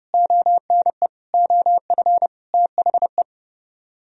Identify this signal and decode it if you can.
Morse code, heard as a keyed tone, about 36 wpm, 700 Hz. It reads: ONE OF THE
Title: Speed-Racing - Sets of 4 Words 30wpm